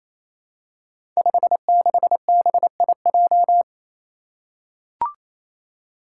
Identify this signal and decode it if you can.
Morse code, heard as a keyed tone, about 28 wpm, 700 Hz. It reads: 56BIJ
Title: Five Challenging Letter-Number Combinations 28wpm